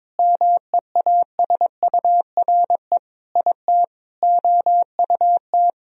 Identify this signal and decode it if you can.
Morse code, heard as a keyed tone, about 22 wpm, 700 Hz. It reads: MEASURE IT OUT
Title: Sentences from Top 1000 Words - Review Part 2 - 22wpm